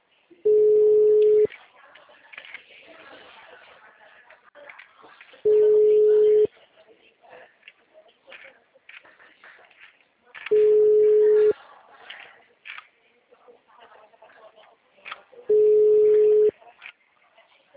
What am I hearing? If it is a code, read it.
Ringback tone (Europe, 425 Hz), with speech.
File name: out- -2027